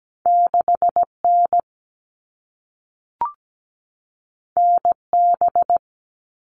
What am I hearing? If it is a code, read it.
Morse code, heard as a keyed tone, about 17 wpm, 700 Hz. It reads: 6N NB
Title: All Cut Numbers - 2-Digit 17wpm